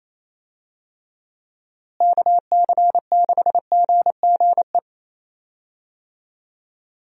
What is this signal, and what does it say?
Morse code, heard as a keyed tone, about 28 wpm, 700 Hz. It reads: KC6GGE
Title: USA Call Signs - 6 Characters - Repeated 3x - 28wpm